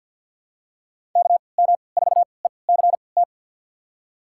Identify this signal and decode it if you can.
Morse code, heard as a keyed tone, about 47 wpm, spaced as for 28 wpm, 700 Hz. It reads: KM4ECT
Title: USA Call Signs - 3x Character Spacing - 50wpm